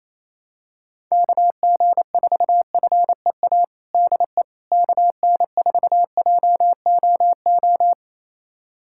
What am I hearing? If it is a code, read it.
Morse code, heard as a keyed tone, about 28 wpm, 700 Hz. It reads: KG4FEA DE KN4JOO
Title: USA Callsign DE Callsign - 6 Characters - 28wpm